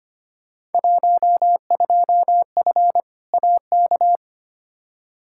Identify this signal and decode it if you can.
Morse code, heard as a keyed tone, about 25 wpm, 700 Hz. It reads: 12F AK
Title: Field Day Exchanges Encore - 25wpm